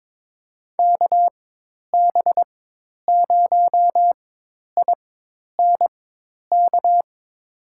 Morse code, heard as a keyed tone, about 21 wpm, spaced as for 10 wpm, 700 Hz. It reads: KB0INK
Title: USA Call Signs - 4x Character Spacing - 22wpm